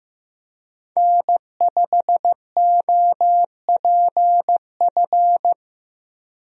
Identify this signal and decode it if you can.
Morse code, heard as a keyed tone, about 15 wpm, 700 Hz. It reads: N5OPF
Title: USA Call Signs - 5 Characters - Repeated 2x - 15wpm